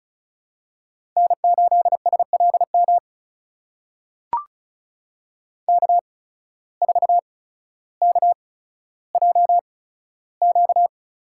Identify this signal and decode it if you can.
Morse code, heard as a keyed tone, about 36 wpm, spaced as for 15 wpm, 700 Hz. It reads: N8SLM K4KJQ
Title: USA Call Signs - 5 Characters - 8x Character Spacing - 35wpm